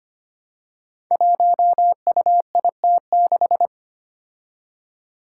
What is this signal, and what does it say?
Morse code, heard as a keyed tone, about 25 wpm, 700 Hz. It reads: 1UIT6
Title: Five Challenging Letter-Number Combinations 25wpm